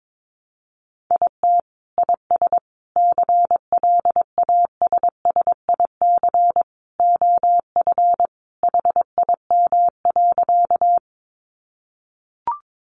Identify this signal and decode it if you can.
Morse code, heard as a keyed tone, about 22 wpm, 700 Hz. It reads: IT IS CLASSIC OF HIM.
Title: Sentences from Top 1000 Words 22wpm